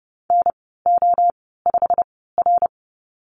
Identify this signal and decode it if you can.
Morse code, heard as a keyed tone, about 30 wpm, spaced as for 17 wpm, 700 Hz. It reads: NO5R
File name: Lesson 15g - Number 5 - Call Signs 30wpm